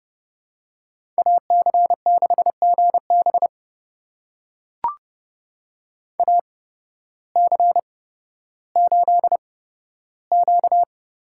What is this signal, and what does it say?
Morse code, heard as a keyed tone, about 30 wpm, spaced as for 14 wpm, 700 Hz. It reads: AC6GB AC8Q
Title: USA Call Signs - 5 Characters - 8x Character Spacing - 30wpm